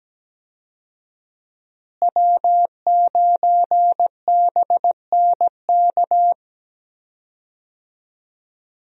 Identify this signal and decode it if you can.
Morse code, heard as a keyed tone, about 17 wpm, 700 Hz. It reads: W9BNK
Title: USA Call Signs - 5 Characters - Repeated 2x - 17wpm